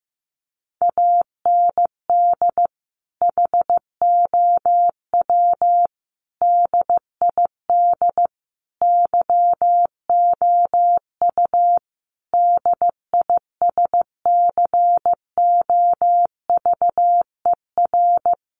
Morse code, heard as a keyed tone, about 15 wpm, 700 Hz. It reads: AND HOW DID YOU DISCOVER
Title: Sentences from Top 1100 Words 15wpm